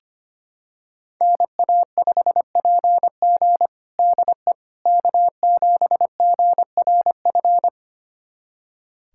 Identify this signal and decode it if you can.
Morse code, heard as a keyed tone, about 25 wpm, 700 Hz. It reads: NA5PG DE K7GRF
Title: USA Callsign DE Callsign - 5 Characters - 25wpm